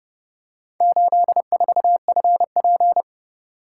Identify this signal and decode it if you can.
Morse code, heard as a keyed tone, about 27 wpm, 700 Hz. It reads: 84FP